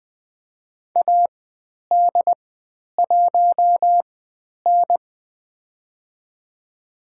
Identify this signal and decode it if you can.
Morse code, heard as a keyed tone, about 20 wpm, spaced as for 10 wpm, 700 Hz. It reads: AD1N
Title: 109 Lesson 18g - Letter m - Call Signs 20wpm